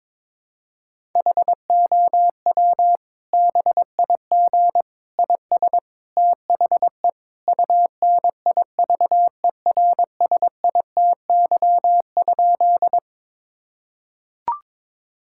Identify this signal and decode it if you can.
Morse code, heard as a keyed tone, about 22 wpm, 700 Hz. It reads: HOW BIG IS THE UNIVERSITY?
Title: Sentences from Top 1000 Words - Review Part 5 - 22wpm